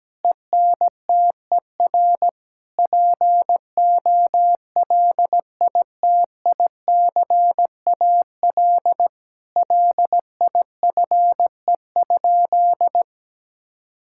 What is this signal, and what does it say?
Morse code, heard as a keyed tone, about 17 wpm, 700 Hz. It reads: ENTER POLITICAL LIFE?
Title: Sentences from Top 1000 Words - Review Part 5 - 17wpm